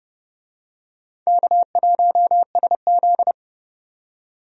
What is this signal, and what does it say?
Morse code, heard as a keyed tone, about 30 wpm, 700 Hz. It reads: K1SZ
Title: USA Call Signs - 4 Characters - 30wpm